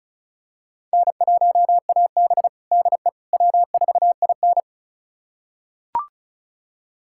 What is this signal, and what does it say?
Morse code, heard as a keyed tone, about 36 wpm, 700 Hz. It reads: N1AB DE W4IN
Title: USA Callsign DE Callsign - 4 Characters - 35wpm